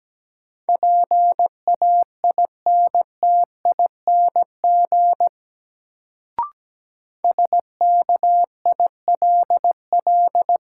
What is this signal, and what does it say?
Morse code, heard as a keyed tone, about 17 wpm, 700 Hz. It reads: PAINTING SKILL
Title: Sentences from Top 1200 Words - New Word Drill Alt - 17wpm